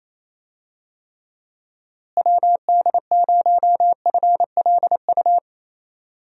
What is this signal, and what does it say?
Morse code, heard as a keyed tone, about 28 wpm, 700 Hz. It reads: WD0FLU
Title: USA Call Signs - 6 Characters - Repeated 3x - 28wpm